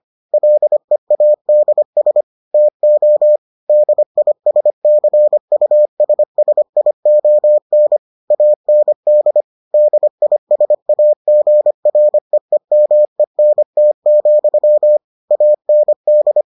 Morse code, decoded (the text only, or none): LEADS TO DISCUSSION AND DISAGREEMENT, AND